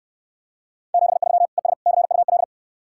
Morse code, heard as a keyed tone, about 69 wpm, 700 Hz. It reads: 73 ES CUL